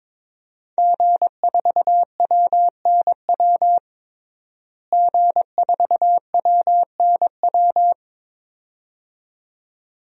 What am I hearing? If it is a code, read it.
Morse code, heard as a keyed tone, about 22 wpm, 700 Hz. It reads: G4WNW G4WNW
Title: DXCC-100 Call Signs - 5 Characters - Repeated 2x - 22wpm